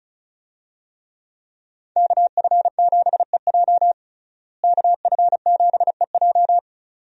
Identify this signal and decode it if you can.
Morse code, heard as a keyed tone, about 34 wpm, 700 Hz. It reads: KF7EJ KF7EJ
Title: USA Call Signs - 5 Characters - Repeated 3x - 35wpm